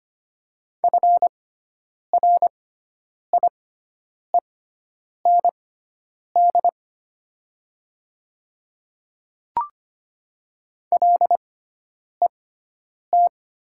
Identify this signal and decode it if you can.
Morse code, heard as a keyed tone, about 25 wpm, spaced as for 9 wpm, 700 Hz. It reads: FRIEND LET
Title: 6x Character Spacing - Top 300 Words - 25wpm